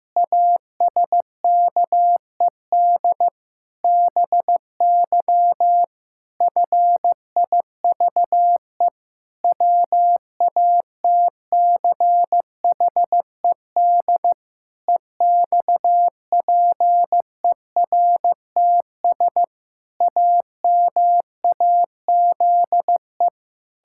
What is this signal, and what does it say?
Morse code, heard as a keyed tone, about 15 wpm, 700 Hz. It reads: ASKED BY FIVE WATCHED EXPERTS AMAZE
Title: Pangram Challenge v2 - 15wpm